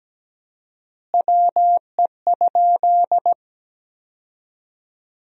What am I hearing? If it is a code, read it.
Morse code, heard as a keyed tone, about 17 wpm, 700 Hz. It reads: WE?
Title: 4x Word-Spacing - Sentences from Top 200 Words - 17wpm